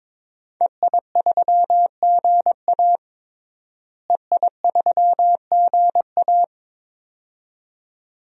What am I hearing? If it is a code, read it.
Morse code, heard as a keyed tone, about 22 wpm, 700 Hz. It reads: EI3GA EI3GA
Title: DXCC-100 Call Signs - 5 Characters - Repeated 2x - 22wpm